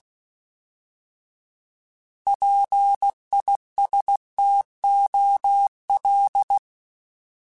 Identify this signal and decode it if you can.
Morse code, heard as a keyed tone, about 16 wpm, 800 Hz. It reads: PISTOL
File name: Alphabetical / Alphabetical mp3s / Ps @15